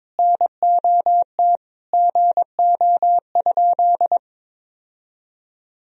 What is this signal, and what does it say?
Morse code, heard as a keyed tone, about 22 wpm, 700 Hz. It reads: NOT GO?
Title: Sentences from Top 1000 Words - Review Part 4 - 22wpm